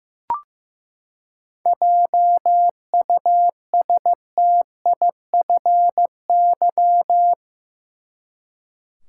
Morse code, heard as a keyed tone, about 15 wpm, 700 Hz. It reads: JUSTIFY
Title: Seven Letter Words - 15wpm